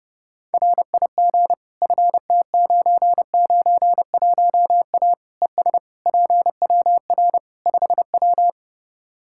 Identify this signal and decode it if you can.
Morse code, heard as a keyed tone, about 32 wpm, 700 Hz. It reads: RIG FT991A ES PWR 5W
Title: Practice QSO - Exchange 2 - Info - 30wpm